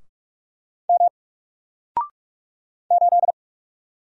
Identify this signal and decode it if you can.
Morse code, heard as a keyed tone, about 46 wpm, 700 Hz. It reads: M 8
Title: Single Letter-Number Alt 45wpm